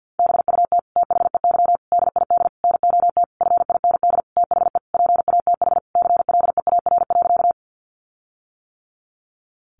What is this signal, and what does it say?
Morse code, heard as a keyed tone, about 48 wpm, 700 Hz. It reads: BUT THEY DID NOT FIND THE PATH CLEAR.
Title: Sentences from Top 1400 Words 50wpm